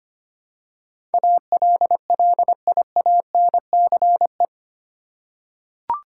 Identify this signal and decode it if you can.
Morse code, heard as a keyed tone, about 25 wpm, 700 Hz. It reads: ALLIANCE
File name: Eight Letter Words - 25wpm